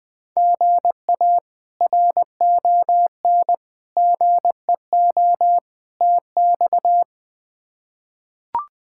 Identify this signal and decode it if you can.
Morse code, heard as a keyed tone, about 20 wpm, 700 Hz. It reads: GA RON GEO TX
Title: K1USN SST QSO Practice - 20wpm